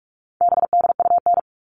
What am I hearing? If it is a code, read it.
Morse code, heard as a keyed tone, about 44 wpm, 700 Hz. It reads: BDUN
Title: All Cut Numbers - 4-Digit 45wpm